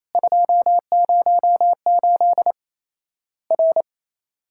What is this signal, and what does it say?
Morse code, heard as a keyed tone, about 28 wpm, 700 Hz. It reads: 208 R
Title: Worked All Europe DX Contest - QTC Exchange - 28wpm